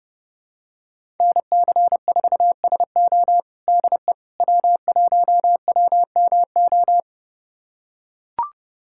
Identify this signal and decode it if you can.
Morse code, heard as a keyed tone, about 30 wpm, 700 Hz. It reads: NC4SO DE W1WMO
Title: USA Callsign DE Callsign - 5 Characters - 30wpm